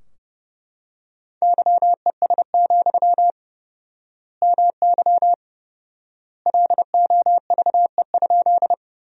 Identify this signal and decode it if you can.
Morse code, heard as a keyed tone, about 30 wpm, 700 Hz. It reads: YES, MY LOVE?
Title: 4x Word-Spacing - Sentences from Top 500 Words - 30wpm